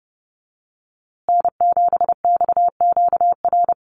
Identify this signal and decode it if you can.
Morse code, heard as a keyed tone, about 30 wpm, 700 Hz. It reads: N7XQR
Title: USA Call Signs Repeated 2x 30wpm